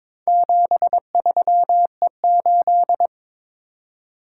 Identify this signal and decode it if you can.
Morse code, heard as a keyed tone, about 22 wpm, 700 Hz. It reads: 73E8
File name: practice / 22wpm / Four Letter-Number Combinations - 22wpm